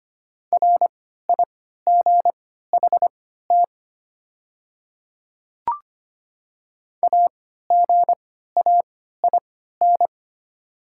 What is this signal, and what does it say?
Morse code, heard as a keyed tone, about 25 wpm, spaced as for 14 wpm, 700 Hz. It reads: RIGHT AGAIN
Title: 3x Character Spacing - Top 200 Words - 25wpm